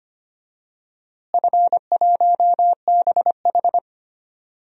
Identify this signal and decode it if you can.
Morse code, heard as a keyed tone, about 25 wpm, 700 Hz. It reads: F1BH
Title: DXCC-100 Call Signs - 4 Characters - Repeated 2x - 25wpm